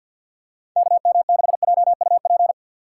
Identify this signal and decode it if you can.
Morse code, heard as a keyed tone, about 50 wpm, 700 Hz. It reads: KM6JUP
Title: USA Call Signs - 6 Characters - 6x Character Spacing - 50wpm